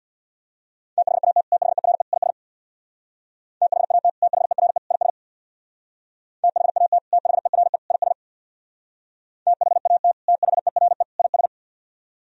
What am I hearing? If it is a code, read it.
Morse code, heard as a keyed tone, about 64 wpm, 700 Hz. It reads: THAT THERE IS THAT THERE IS THAT THERE IS THAT THERE IS